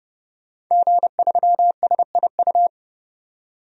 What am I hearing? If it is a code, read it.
Morse code, heard as a keyed tone, about 30 wpm, 700 Hz. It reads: G3SIU
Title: DXCC-100 Call Signs - 5 Characters - 30wpm